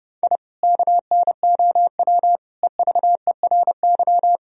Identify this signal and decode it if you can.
Morse code, heard as a keyed tone, about 30 wpm, 700 Hz. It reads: I KNOW EVERY
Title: Sentences from Top 1000 Words - Review Part 3 - 30wpm